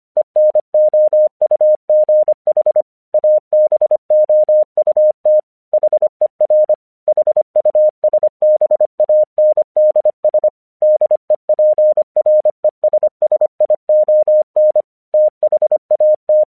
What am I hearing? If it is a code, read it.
Morse code, heard as a keyed tone, about 25 wpm, 610 Hz. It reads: ENOUGH ABOUT HER HUSBANDS DEPRESSION THAT
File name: 1 English 25wpm / AQC14